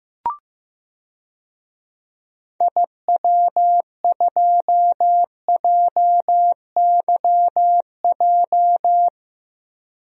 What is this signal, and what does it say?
Morse code, heard as a keyed tone, about 15 wpm, 700 Hz. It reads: IW2JYJ
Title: DXCC-100 Call Signs - 6 Characters - 15wpm